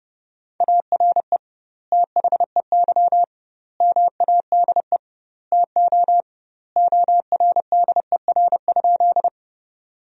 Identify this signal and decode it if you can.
Morse code, heard as a keyed tone, about 30 wpm, 700 Hz. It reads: ARE THEY MADE TO ORDER?
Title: 2x Word-Spacing - Sentences from Top 400 Words - 30wpm